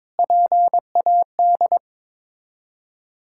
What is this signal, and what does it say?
Morse code, heard as a keyed tone, about 22 wpm, 690 Hz. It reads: PAD
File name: Difficult Words with Hints - 22wpm